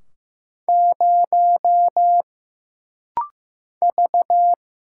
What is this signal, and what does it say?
Morse code, heard as a keyed tone, about 15 wpm, 700 Hz. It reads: 0 V
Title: Single Letter-Number Alt - Rapid-Fire 15wpm